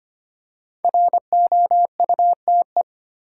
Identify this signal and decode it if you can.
Morse code, heard as a keyed tone, about 25 wpm, 700 Hz. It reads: ROUTE
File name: 5 Letter Words - First 3 Letters Given - 25wpm